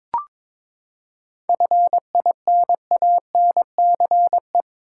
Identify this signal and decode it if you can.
Morse code, heard as a keyed tone, about 22 wpm, 700 Hz. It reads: FINANCE
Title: Seven Letter Words - 22wpm